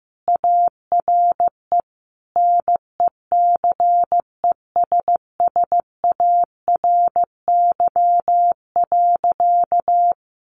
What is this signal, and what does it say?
Morse code, heard as a keyed tone, about 15 wpm, 700 Hz. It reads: ARE NECESSARY.
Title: Sentences from Top 1400 Words 15wpm